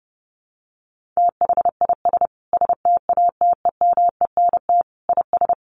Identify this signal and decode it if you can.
Morse code, heard as a keyed tone, about 30 wpm, 700 Hz. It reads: THIS STATEMENT IS
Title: Sentences from Top 1300 Words 30wpm